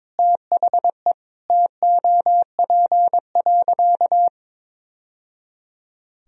Morse code, heard as a keyed tone, about 22 wpm, 700 Hz. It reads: THE TOP.
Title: Sentences from Top 1000 Words - Review Part 2 - 22wpm